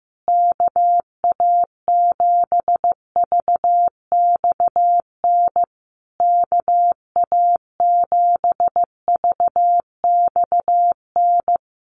Morse code, heard as a keyed tone, about 15 wpm, 700 Hz. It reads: KA7VXN KA7VXN
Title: USA Call Signs Repeated 3x 15wpm